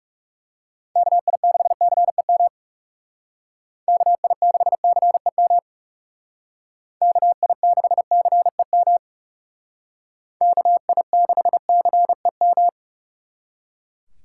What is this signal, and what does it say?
Morse code, heard as a keyed tone, about 38 wpm, 700 Hz. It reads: KI6CEM KI6CEM KI6CEM KI6CEM